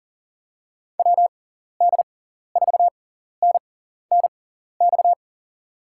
Morse code, heard as a keyed tone, about 38 wpm, spaced as for 14 wpm, 700 Hz. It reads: WD4NNX
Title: USA Call Signs - 6 Characters - 6x Character Spacing - 40wpm